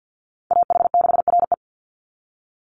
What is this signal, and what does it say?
Morse code, heard as a keyed tone, about 50 wpm, 700 Hz. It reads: AH6RE
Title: Call Signs - 50wpm